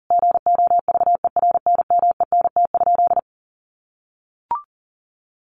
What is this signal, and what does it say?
Morse code, heard as a keyed tone, about 40 wpm, 700 Hz. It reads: GOVERNMENT?
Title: Sentences from Top 1400 Words 40wpm